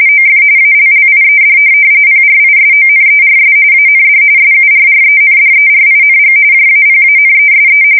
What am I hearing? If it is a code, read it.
RTTY, about 45.45 baud. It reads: OF ALLIED PRISONERS AND SURRENDERING JAPANESE
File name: fsk170 / USS-New-Jersey